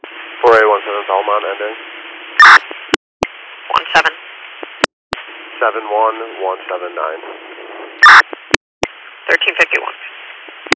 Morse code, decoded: none

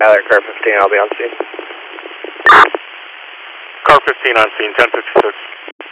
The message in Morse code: none

